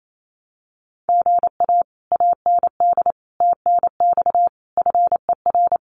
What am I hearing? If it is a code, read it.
Morse code, heard as a keyed tone, about 28 wpm, 700 Hz. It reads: GA AND TNX FER
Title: Common QSO Phrases Encore 28wpm